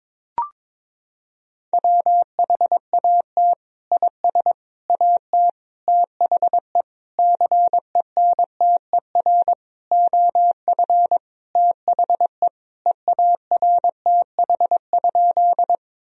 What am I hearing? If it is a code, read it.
Morse code, heard as a keyed tone, about 22 wpm, 700 Hz. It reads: WHAT IS AT THE CENTER OF THE EARTH?
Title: Sentences from Top 1000 Words - Review Part 2 - 22wpm